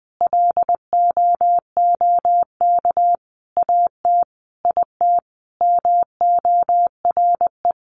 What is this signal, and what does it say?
Morse code, heard as a keyed tone, about 20 wpm, 700 Hz. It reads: LOOK AT IT MORE
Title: Sentences from Top 1400 Words 20wpm